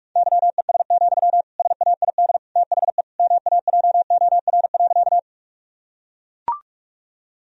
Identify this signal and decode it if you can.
Morse code, heard as a keyed tone, about 41 wpm, 700 Hz. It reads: YES, SAID THE MAJOR.
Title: Sentences from Top 1000 Words - Review Part 4 - 45wpm